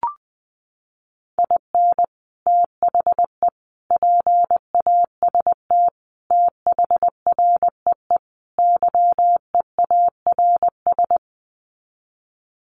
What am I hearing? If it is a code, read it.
Morse code, heard as a keyed tone, about 20 wpm, 700 Hz. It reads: IN THE PAST THREE YEARS
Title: Sets of 5 Words Encore2 20wpm